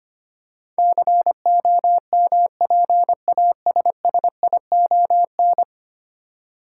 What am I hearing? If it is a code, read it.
Morse code, heard as a keyed tone, about 25 wpm, 700 Hz. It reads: COMPASSION
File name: Ten Letter Words - 25wpm